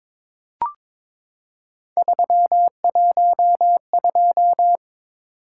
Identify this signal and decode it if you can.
Morse code, heard as a keyed tone, about 22 wpm, 700 Hz. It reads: 312
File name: practice / 22wpm / Three Letter-Number Combinations - 22wpm